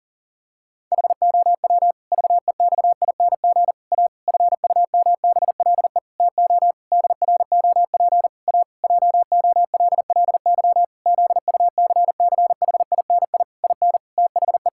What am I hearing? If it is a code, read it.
Morse code, heard as a keyed tone, about 40 wpm, 700 Hz. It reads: HOW VEXING A FUMBLE TO DROP A JOLLY ZUCCHINI IN THE
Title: Pangram Challenge v2 - 40wpm